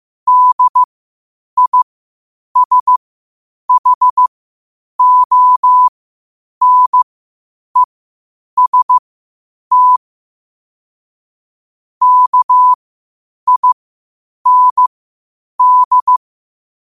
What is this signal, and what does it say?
Morse code, heard as a keyed tone, about 15 wpm, spaced as for 8 wpm, 1000 Hz. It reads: DISHONEST KIND
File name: qotdfiles / 8wpm